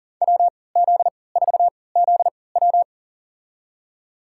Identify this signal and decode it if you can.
Morse code, heard as a keyed tone, about 40 wpm, spaced as for 23 wpm, 700 Hz. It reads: WZ4ZW
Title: USA Call Signs - 5 Characters - 3x Character Spacing - 40wpm